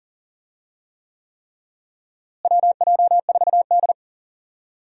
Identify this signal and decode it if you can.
Morse code, heard as a keyed tone, about 40 wpm, 700 Hz. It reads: WJ4D
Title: USA Call Signs - 4 Characters - Repeated 3x - 40wpm